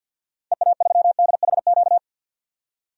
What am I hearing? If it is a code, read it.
Morse code, heard as a keyed tone, about 47 wpm, 700 Hz. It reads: EA3DHQ
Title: DXCC-100 Call Signs - 6 Characters - 50wpm